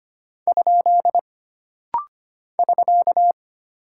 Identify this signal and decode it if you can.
Morse code, heard as a keyed tone, about 25 wpm, 700 Hz. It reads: ? <SK>
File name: Single Punctuation-Prosign - Rapid-Fire - 25wpm